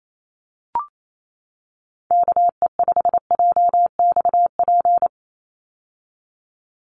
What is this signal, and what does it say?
Morse code, heard as a keyed tone, about 28 wpm, 700 Hz. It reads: KE5JXP
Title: USA Call Signs 28wpm